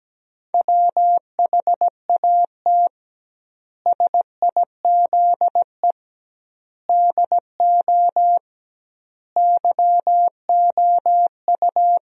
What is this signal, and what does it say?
Morse code, heard as a keyed tone, about 17 wpm, 700 Hz. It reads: WHAT SIZE DO YOU
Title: 2x Word-Spacing - Sentences from Top 400 Words - 17wpm